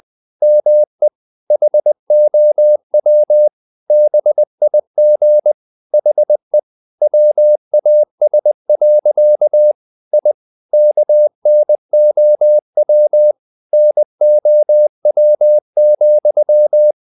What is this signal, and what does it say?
Morse code, heard as a keyed tone, about 20 wpm, 600 Hz. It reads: ME HOW BIG HE WAS. I KNOW NOW,
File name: the old man and the sea / the old man and the sea part 1 20wpm